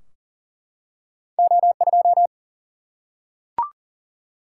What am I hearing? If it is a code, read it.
Morse code, heard as a keyed tone, about 40 wpm, 700 Hz. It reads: O2
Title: Single-Letter Single-Number 40wpm